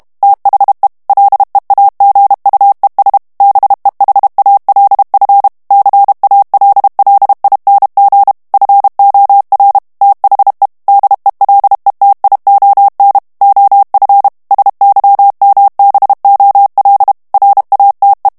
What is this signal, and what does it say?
Morse code, heard as a keyed tone, about 32 wpm, 800 Hz. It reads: THE LEAGUES BEHALF CALLING FOR THE DELETION OF SYMBOL RATE
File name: morse mp3 / ANGLAIS texte clair 3 vitesse 30 mots minute